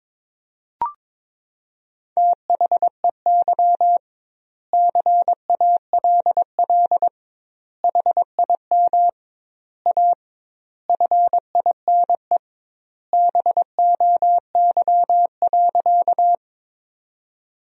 Morse code, heard as a keyed tone, about 22 wpm, 700 Hz. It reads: THEY CALL HIM A FINE BOY.
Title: 2x Word-Spacing - Sentences from Top 500 Words - 22wpm